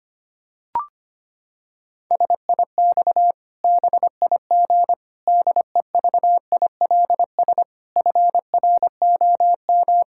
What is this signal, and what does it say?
Morse code, heard as a keyed tone, about 25 wpm, 700 Hz. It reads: SIX BIG DEVILS FROM
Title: Pangram Challenge v2 - 25wpm